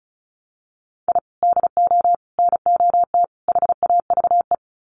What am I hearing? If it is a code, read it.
Morse code, heard as a keyed tone, about 34 wpm, 700 Hz. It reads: I DO NOT HAVE
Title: Sets of 4 Words 35wpm